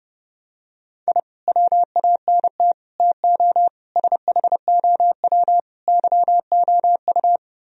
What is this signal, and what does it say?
Morse code, heard as a keyed tone, about 30 wpm, 700 Hz. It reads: I WANT TO SHOW YOU